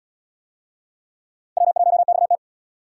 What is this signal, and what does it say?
Morse code, heard as a keyed tone, about 65 wpm, 700 Hz. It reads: W1XT